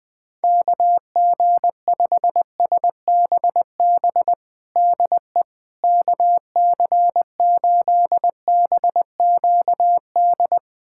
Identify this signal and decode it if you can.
Morse code, heard as a keyed tone, about 21 wpm, 700 Hz. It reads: KG5SBB DE KC8BQD
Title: USA Callsign DE Callsign - 6 Characters - 20wpm